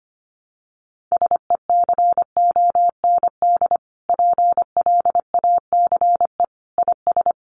Morse code, heard as a keyed tone, about 25 wpm, 700 Hz. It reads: SECOND PLACE IS
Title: Sentences from Top 1500 Words 25wpm